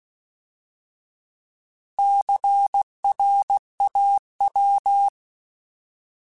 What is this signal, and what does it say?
Morse code, heard as a keyed tone, about 16 wpm, 800 Hz. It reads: CRAW
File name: Alphabetical / Alphabetical mp3s / Cs @15